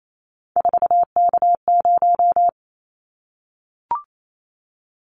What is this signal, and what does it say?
Morse code, heard as a keyed tone, about 28 wpm, 700 Hz. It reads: 4K0
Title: Callsign Prefixes for Top 100 DXCC - 28wpm